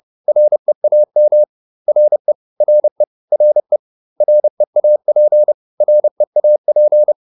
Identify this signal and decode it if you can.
Morse code, heard as a keyed tone, about 30 wpm, 600 Hz. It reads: REAM RE RE RE REAP REAP
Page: advanced-practice-files